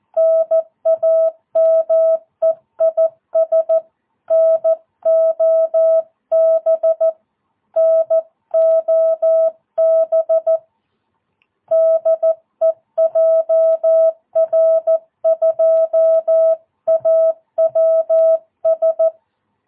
Morse code, heard as a keyed tone, about 14 wpm, 660 Hz. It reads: NAME IS NOB NOB DE JR2AWS
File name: jh8cbh / 2022cw